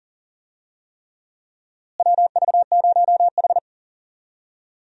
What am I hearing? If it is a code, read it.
Morse code, heard as a keyed tone, about 37 wpm, 700 Hz. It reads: WV0H
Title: USA Call Signs - 4 Characters - Repeated 3x - 40wpm